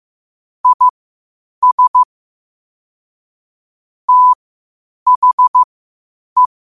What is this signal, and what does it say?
Morse code, heard as a keyed tone, about 15 wpm, spaced as for 8 wpm, 1000 Hz. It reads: IS THE
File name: qotdfiles / 8wpm